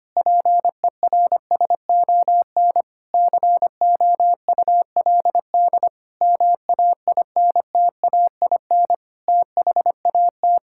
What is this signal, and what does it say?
Morse code, heard as a keyed tone, about 25 wpm, 700 Hz. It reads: PERSON COULD MAINTAIN THAT
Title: Sentences from Top 1000 Words - Review Part 5 - 25wpm